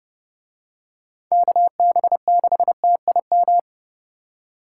Morse code, heard as a keyed tone, about 30 wpm, 700 Hz. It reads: KB6TIM